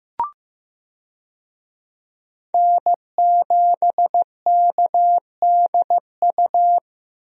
Morse code, heard as a keyed tone, about 15 wpm, 700 Hz. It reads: N7KDU
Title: USA Call Signs - 5 Characters - 15wpm